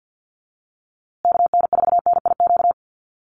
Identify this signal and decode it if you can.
Morse code, heard as a keyed tone, about 50 wpm, 700 Hz. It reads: KN4NIQ
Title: USA Call Signs Repeated 3x 50wpm